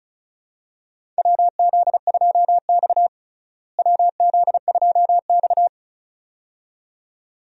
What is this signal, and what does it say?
Morse code, heard as a keyed tone, about 34 wpm, 700 Hz. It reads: WZ2X WZ2X
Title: USA Call Signs - 4 Characters - Repeated 2x - 35wpm